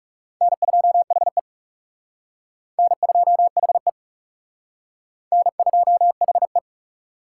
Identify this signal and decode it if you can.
Morse code, heard as a keyed tone, about 38 wpm, 700 Hz. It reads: N2HE N2HE N2HE
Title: Speed Racing - USA 1x2 Call Signs 30wpm